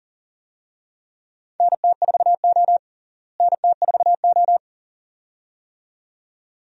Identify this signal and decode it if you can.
Morse code, heard as a keyed tone, about 40 wpm, 700 Hz. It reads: NT4O NT4O
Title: USA Call Signs - 4 Characters - Repeated 2x - 40wpm